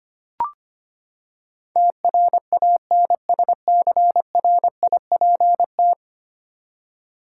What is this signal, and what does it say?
Morse code, heard as a keyed tone, about 25 wpm, 700 Hz. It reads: TRANSCRIPT
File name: Ten Letter Words - 25wpm